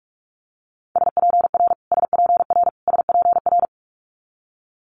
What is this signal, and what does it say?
Morse code, heard as a keyed tone, about 46 wpm, 700 Hz. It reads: SPR SPR SPR
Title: Multi-letter Phonemes Repeated 3x 45wpm